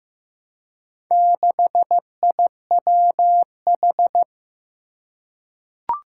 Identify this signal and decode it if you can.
Morse code, heard as a keyed tone, about 15 wpm, 700 Hz. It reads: 6IWH
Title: Four Challenging Letter-Number Combinations 15wpm